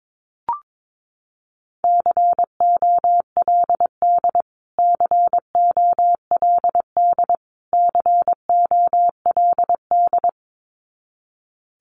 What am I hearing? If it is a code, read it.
Morse code, heard as a keyed tone, about 22 wpm, 700 Hz. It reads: COLD COLD COLD
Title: Top 500 Words Repeated 22wpm